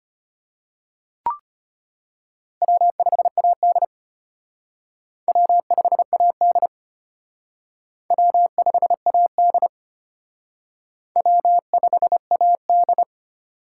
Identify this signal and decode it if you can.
Morse code, heard as a keyed tone, about 31 wpm, 700 Hz. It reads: W5AD W5AD W5AD W5AD
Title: Speed Racing - USA Call Signs 25wpm